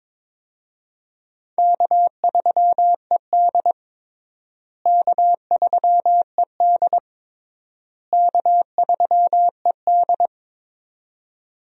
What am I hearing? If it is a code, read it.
Morse code, heard as a keyed tone, about 22 wpm, 700 Hz. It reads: K3ED K3ED K3ED
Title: USA Call Signs - 4 Characters - Repeated 3x - 22wpm